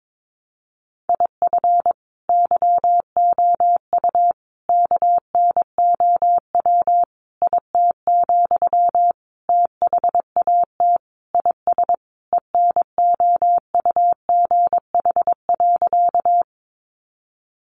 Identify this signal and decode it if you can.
Morse code, heard as a keyed tone, about 22 wpm, 700 Hz. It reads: IF YOU KNOW IT, THAT IS ENOUGH.
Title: Sentences from Top 400 Words 22wpm